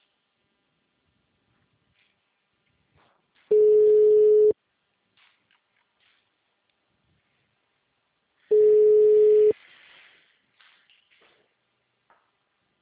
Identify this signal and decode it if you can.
Ringback tone (Europe, 425 Hz).